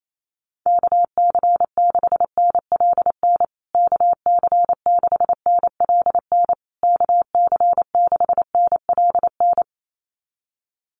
Morse code, heard as a keyed tone, about 28 wpm, 700 Hz. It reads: KC6NLN KC6NLN KC6NLN
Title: USA Call Signs Repeated 3x 28wpm